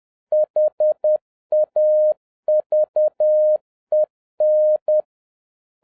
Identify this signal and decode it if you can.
Morse code, heard as a keyed tone, about 10 wpm, 610 Hz. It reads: HAVEN
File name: 2011-07-mp3 / n46 JOHN K3TN 10wpm